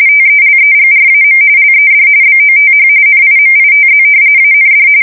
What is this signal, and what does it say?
RTTY, about 45.45 baud. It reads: TELEGRAM LED BOTH NIMITZ AND
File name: fsk170 / battleshipnj